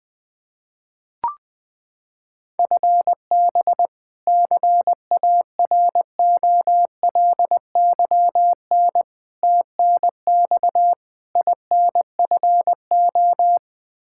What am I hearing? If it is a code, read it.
Morse code, heard as a keyed tone, about 20 wpm, 700 Hz. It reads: FB CAROLYN TNX INFO
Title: Practice QSO - Exchange 3 - More Info - 20wpm